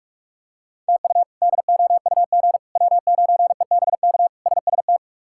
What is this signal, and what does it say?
Morse code, heard as a keyed tone, about 42 wpm, 700 Hz. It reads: TU DOUG W9EBK SST
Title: K1USN SST QSO Practice - 45wpm